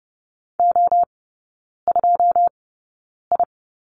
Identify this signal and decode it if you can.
Morse code, heard as a keyed tone, about 30 wpm, spaced as for 9 wpm, 700 Hz. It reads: O2I
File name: 081 Lesson 14e - Number 2 - Three Characters 30wpm